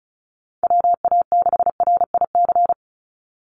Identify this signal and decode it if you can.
Morse code, heard as a keyed tone, about 34 wpm, 700 Hz. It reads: WA6RIC
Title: NA SOTA Chaser Callsigns 35wpm